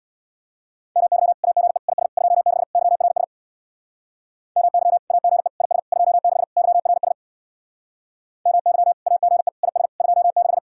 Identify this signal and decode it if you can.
Morse code, heard as a keyed tone, about 65 wpm, 700 Hz. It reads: MY AGE IS 26 YRS MY AGE IS 26 YRS MY AGE IS 26